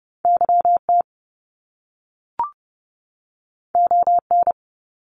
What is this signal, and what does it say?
Morse code, heard as a keyed tone, about 30 wpm, 700 Hz. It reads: Y T O N
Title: Top Two Letter Combinations in Words 30wpm